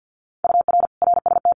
Morse code, heard as a keyed tone, about 50 wpm, 700 Hz. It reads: UR RST
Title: Common QSO Phrases Encore 50wpm